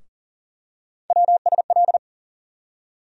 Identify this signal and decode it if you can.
Morse code, heard as a keyed tone, about 40 wpm, 700 Hz. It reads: WSL
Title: USA Call Sign Postfixes - 3 Letters 40wpm